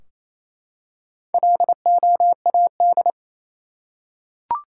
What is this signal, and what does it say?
Morse code, heard as a keyed tone, about 28 wpm, 700 Hz. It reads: LOAD
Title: Four Letter Words - 28wpm